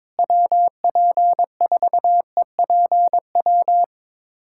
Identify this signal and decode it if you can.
Morse code, heard as a keyed tone, about 22 wpm, 700 Hz. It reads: WP4EPW
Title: USA Callsign DE Callsign - 6 Characters - 22wpm